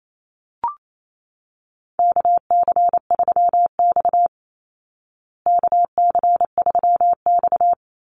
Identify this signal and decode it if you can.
Morse code, heard as a keyed tone, about 28 wpm, 700 Hz. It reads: KC3X KC3X
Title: Call Signs Repeated 2x - 28wpm